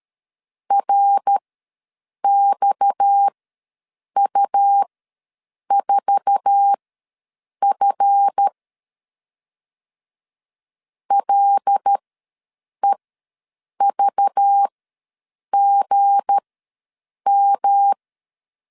Morse code, heard as a keyed tone, about 13 wpm, spaced as for 7 wpm, 800 Hz. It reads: RXU4F LEVGM
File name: LEZIONI / 8 YJ45 / 8 YJ4 ws 12wpm 3249